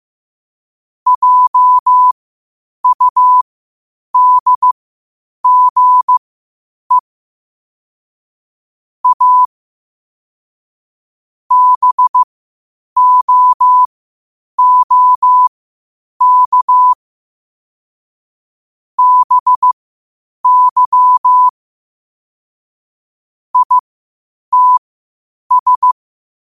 Morse code, heard as a keyed tone, about 15 wpm, spaced as for 8 wpm, 1000 Hz. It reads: JUDGE A BOOK BY ITS
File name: qotdfiles / 8wpm